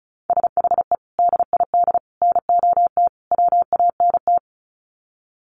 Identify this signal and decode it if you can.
Morse code, heard as a keyed tone, about 34 wpm, 700 Hz. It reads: SHE DID NOT WANT
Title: Sets of 4 Words 35wpm